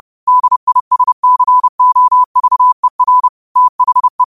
Morse code, heard as a keyed tone, about 30 wpm, 1000 Hz. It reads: DISCOVER THE